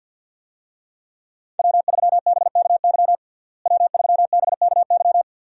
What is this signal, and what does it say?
Morse code, heard as a keyed tone, about 52 wpm, 700 Hz. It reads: W3BKY W3BKY
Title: USA Call Signs - 5 Characters - Repeated 2x - 50wpm